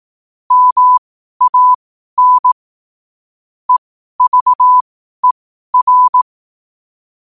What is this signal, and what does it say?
Morse code, heard as a keyed tone, about 18 wpm, spaced as for 12 wpm, 1000 Hz. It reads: MAN EVER
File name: qotdfiles / 12wpm